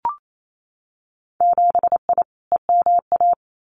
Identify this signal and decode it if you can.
Morse code, heard as a keyed tone, about 28 wpm, 700 Hz. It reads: 7I EMA
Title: Winter Field Day Exchanges Encore 28wpm